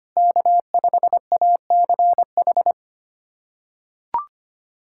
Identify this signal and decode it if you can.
Morse code, heard as a keyed tone, about 25 wpm, 700 Hz. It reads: K5ACH
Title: USA Callsign DE Callsign - 5 Characters - 25wpm